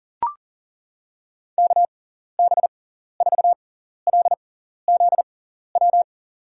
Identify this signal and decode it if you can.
Morse code, heard as a keyed tone, about 37 wpm, spaced as for 14 wpm, 700 Hz. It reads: KB4LZW
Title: USA Call Signs - 6 Characters - 6x Character Spacing - 40wpm